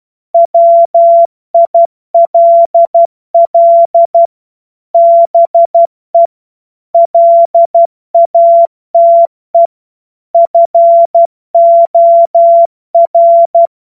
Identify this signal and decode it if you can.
Morse code, heard as a keyed tone, about 12 wpm, 680 Hz. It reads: WILL BE LATE FOR
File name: Morsekurs / Witze